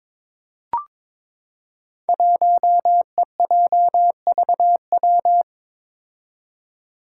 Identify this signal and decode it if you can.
Morse code, heard as a keyed tone, about 22 wpm, 700 Hz. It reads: 1EJVW
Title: Five Challenging Letter-Number Combinations 22wpm